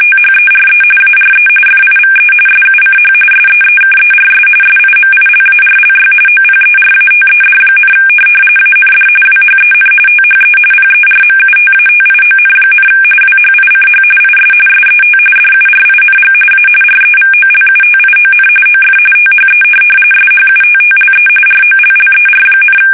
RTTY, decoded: THERE ARE FIVE SUCH SEEKERS SIMILARLY OPERATED, AND WHEN THE PERMUTATION, WHICH IS SET DURING A REVOLUTION OF THE SHUTTLE ROLLER,

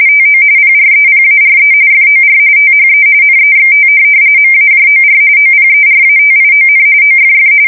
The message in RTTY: TO PREPARE FOR THE NEXT MAJOR OPERATION OFF

